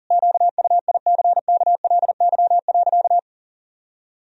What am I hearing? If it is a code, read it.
Morse code, heard as a keyed tone, about 40 wpm, 700 Hz. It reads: QUICKLY.
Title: Sentences from Top 1000 Words - Review Part 1 - 40wpm